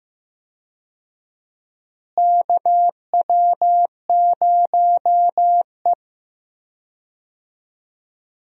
Morse code, heard as a keyed tone, about 15 wpm, 700 Hz. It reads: KW0E
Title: USA Call Signs - 4 Characters - Repeated 3x - 15wpm